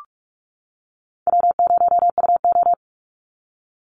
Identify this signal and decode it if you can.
Morse code, heard as a keyed tone, about 44 wpm, 700 Hz. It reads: W0UO
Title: Call Signs - 45wpm